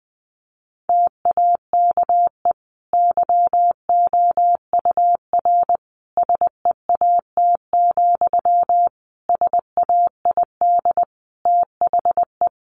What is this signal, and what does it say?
Morse code, heard as a keyed tone, about 20 wpm, 700 Hz. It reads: TAKE YOUR SEAT, SAID THE
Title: Sentences from Top 1500 Words 20wpm